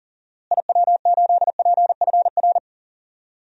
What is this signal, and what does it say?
Morse code, heard as a keyed tone, about 40 wpm, 700 Hz. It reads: IW8PFR
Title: DXCC-100 Call Signs - 6 Characters - Repeated 3x - 40wpm